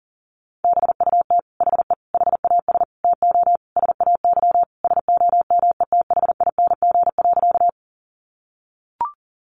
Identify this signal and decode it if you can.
Morse code, heard as a keyed tone, about 40 wpm, 700 Hz. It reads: BUT HE HAS TO SAY SOMETHING.
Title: Sentences from Top 600 Words 40wpm